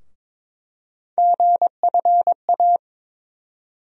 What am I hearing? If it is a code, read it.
Morse code, heard as a keyed tone, about 22 wpm, 700 Hz. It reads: GFA
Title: USA Call Sign Postfixes - 3 Letters 22wpm